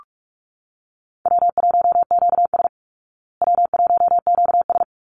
Morse code, heard as a keyed tone, about 44 wpm, 700 Hz. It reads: W1QS W1QS
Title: Call Signs Repeated 3x - 45wpm